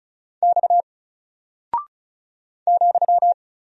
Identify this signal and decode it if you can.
Morse code, heard as a keyed tone, about 36 wpm, 700 Hz. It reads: X ,
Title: Single Character (Letter:Number:Punctuation:Prosign) 35wpm